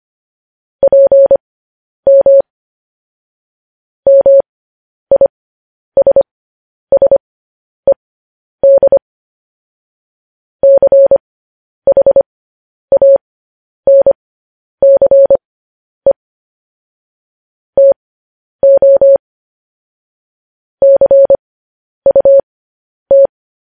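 Morse code, heard as a keyed tone, about 25 wpm, spaced as for 10 wpm, 550 Hz. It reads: PM MISSED CHANCE TO CUT
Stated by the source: Morse code news headlines, likely sent at 10 WPM.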